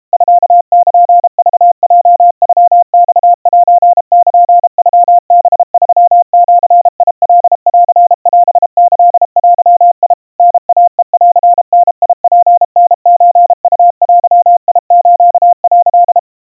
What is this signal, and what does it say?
Morse code (Japanese Wabun code), heard as a keyed tone, about 32 wpm, 700 Hz. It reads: ミルクヲノマセルノハ3ジカンオキデ タイヘンダツタソウデス」
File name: kawa / Morse sample / kamyu 120 cpm 700 Hz Crispy